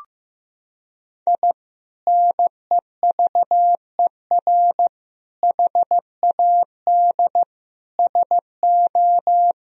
Morse code, heard as a keyed tone, about 15 wpm, 700 Hz. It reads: I NEVER HAD SO
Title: Sentences from Top 1000 Words - Review Part 4 - 15wpm